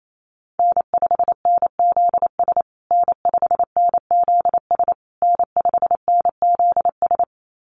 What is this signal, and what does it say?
Morse code, heard as a keyed tone, about 28 wpm, 700 Hz. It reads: N5NZS N5NZS N5NZS
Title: USA Call Signs Repeated 3x 28wpm